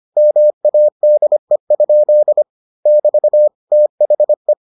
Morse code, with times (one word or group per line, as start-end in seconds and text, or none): 0.2-2.4 MADE?
2.9-3.5 <BT>
3.7-4.5 THE